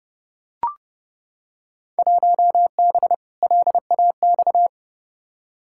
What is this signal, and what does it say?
Morse code, heard as a keyed tone, about 30 wpm, 700 Hz. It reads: 1B LAX
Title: Field Day Exchanges - 30wpm